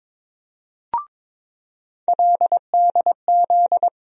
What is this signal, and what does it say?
Morse code, heard as a keyed tone, about 22 wpm, 700 Hz. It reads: LDZ
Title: USA Call Sign Postfixes - 3 Letters 22wpm